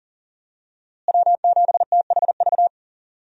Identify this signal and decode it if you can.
Morse code, heard as a keyed tone, about 37 wpm, 700 Hz. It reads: W7THV
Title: USA Call Signs - 5 Characters - Repeated 2x - 40wpm